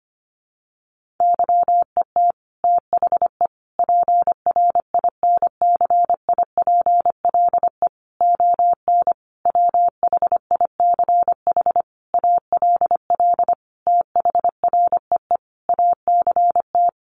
Morse code, heard as a keyed tone, about 25 wpm, 700 Hz. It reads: YET THE PRINCIPLE ON WHICH ALL THREE ACT
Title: Sentences from Top 1400 Words 25wpm